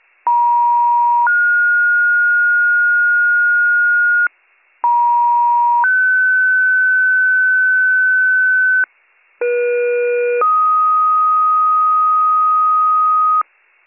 Two-tone page, heard as a one-second tone, then a three-second tone, about 950 then 1430 Hz, about 950 then 1500 Hz, about 480 then 1250 Hz, with noise. It is heard from 0.3 to 13.4 s.